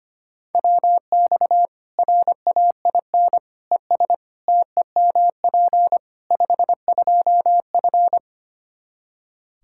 Morse code, heard as a keyed tone, about 25 wpm, 700 Hz. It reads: WX RAIN ES TEMP 52F
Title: Practice QSO - Exchange 2 - Info - 25wpm